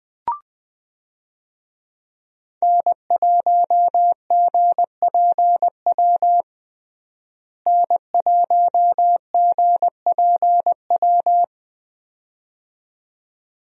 Morse code, heard as a keyed tone, about 20 wpm, 700 Hz. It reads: N1GPW N1GPW
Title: USA Call Signs - 5 Characters - Repeated 2x - 20wpm